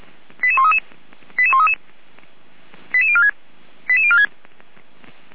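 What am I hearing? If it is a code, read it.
Five-tone selective call (ZVEI), heard as quick runs of five tones, tones of about 70 ms. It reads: 88133 88133 88046 88046
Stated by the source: ZVEI five-tone call received by SDR.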